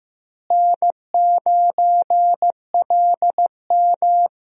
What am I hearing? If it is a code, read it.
Morse code, heard as a keyed tone, about 15 wpm, 700 Hz. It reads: N9LM
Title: USA Call Signs - 4 Characters - Repeated 3x - 15wpm